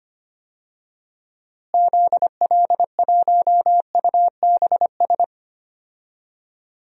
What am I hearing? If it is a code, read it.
Morse code, heard as a keyed tone, about 25 wpm, 700 Hz. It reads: ZL1UBS
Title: DXCC-100 Call Signs - 6 Characters - Repeated 3x - 25wpm